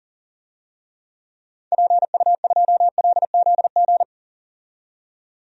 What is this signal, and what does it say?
Morse code, heard as a keyed tone, about 40 wpm, 700 Hz. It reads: PU2LZG
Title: DXCC-100 Call Signs - 6 Characters - 40wpm